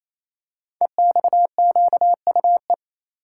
Morse code, heard as a keyed tone, about 28 wpm, 700 Hz. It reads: EXQUE
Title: Code Groups - 5 Letter:Numbers - 28wpm